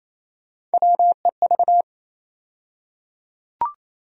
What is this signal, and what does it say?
Morse code, heard as a keyed tone, about 28 wpm, 700 Hz. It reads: WEV
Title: USA Call Sign Postfixes - 3 Letters 28wpm